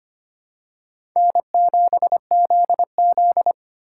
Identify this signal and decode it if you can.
Morse code, heard as a keyed tone, about 25 wpm, 700 Hz. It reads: N7ZZ
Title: USA Call Signs - 4 Characters - 2x Character Spacing - 25wpm